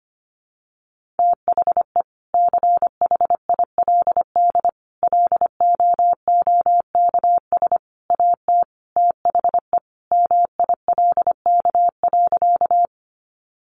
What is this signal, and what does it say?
Morse code, heard as a keyed tone, about 25 wpm, 700 Hz. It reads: THE CHILD LOOKS AT THE MILK.
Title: Additional Easy Phrases 25wpm